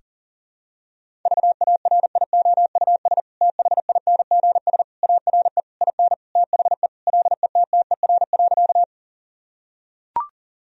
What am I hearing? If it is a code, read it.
Morse code, heard as a keyed tone, about 40 wpm, 700 Hz. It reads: VARIOUS THINGS ARE IN THE LETTER.
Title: Sentences from Top 1000 Words - Review Part 4 - 40wpm